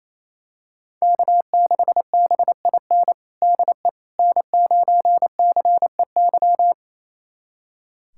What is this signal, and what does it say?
Morse code, heard as a keyed tone, about 28 wpm, 700 Hz. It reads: K6BIN DE N9CEY
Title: USA Callsign DE Callsign - 5 Characters - 28wpm